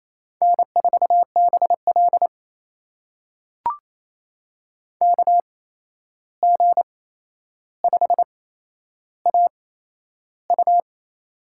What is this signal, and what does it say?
Morse code, heard as a keyed tone, about 28 wpm, spaced as for 11 wpm, 700 Hz. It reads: N4BL KG5AU
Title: USA Call Signs - 8x Character Spacing - 28wpm